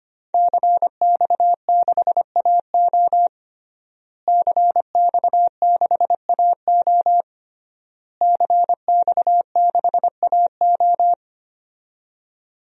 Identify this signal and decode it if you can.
Morse code, heard as a keyed tone, about 25 wpm, 700 Hz. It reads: CX6AO CX6AO CX6AO
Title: DXCC-100 Call Signs - 5 Characters - Repeated 3x - 25wpm